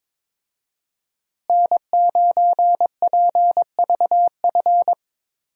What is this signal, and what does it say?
Morse code, heard as a keyed tone, about 22 wpm, 700 Hz. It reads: N9PVF
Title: USA Call Signs - 5 Characters - Repeated 2x - 22wpm